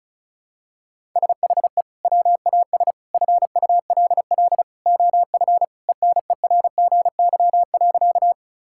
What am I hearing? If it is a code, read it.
Morse code, heard as a keyed tone, about 34 wpm, 700 Hz. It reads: SHE WAS FULL OF ENERGY.
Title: Sentences from Top 1000 Words - Review Part 6 - 35wpm